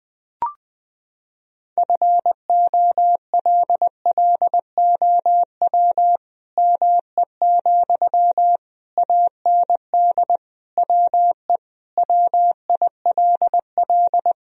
Morse code, heard as a keyed tone, about 20 wpm, 700 Hz. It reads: FOLLOW ME, AND WE WILL
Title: Sentences from Top 1000 Words - Review Part 3 - 20wpm